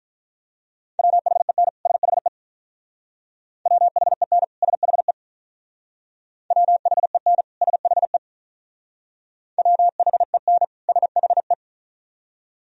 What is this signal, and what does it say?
Morse code, heard as a keyed tone, about 43 wpm, 700 Hz. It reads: WHEN SHE WHEN SHE WHEN SHE WHEN SHE
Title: Speed-Racing - Sets of 2 Words 35wpm